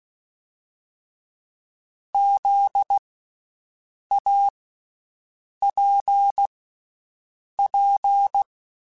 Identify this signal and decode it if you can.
Morse code, heard as a keyed tone, about 16 wpm, spaced as for 6 wpm, 800 Hz. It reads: ZAPP
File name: Special INGs / Special INGs-mp3s / VerySlowToFaster Ing @15